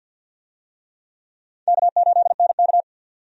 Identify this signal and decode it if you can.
Morse code, heard as a keyed tone, about 47 wpm, 700 Hz. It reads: K8NK
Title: USA Call Signs - 4 Characters - 50wpm